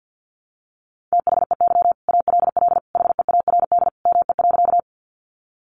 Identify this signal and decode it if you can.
Morse code, heard as a keyed tone, about 50 wpm, 700 Hz. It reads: THEY ALL HEARD ME.
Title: Sentences from Top 400 Words 50wpm